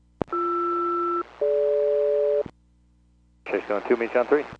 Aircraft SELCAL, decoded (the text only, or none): BR-DG